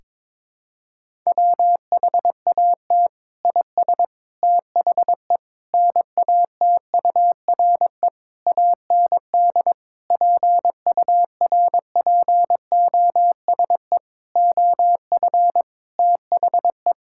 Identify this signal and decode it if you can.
Morse code, heard as a keyed tone, about 22 wpm, 700 Hz. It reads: WHAT IS THE NATURE AND PURPOSE OF THE
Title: Sentences from Top 1000 Words - Review Part 5 - 22wpm